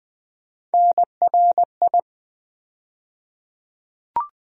Morse code, heard as a keyed tone, about 20 wpm, 700 Hz. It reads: NRI
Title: USA Call Sign Postfixes - 3 Letters 20wpm